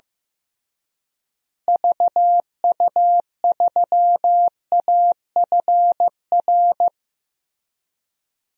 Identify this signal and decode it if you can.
Morse code, heard as a keyed tone, about 16 wpm, 700 Hz. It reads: VU3AFR
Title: DXCC-100 Call Signs - 6 Characters - Repeated 3x - 15wpm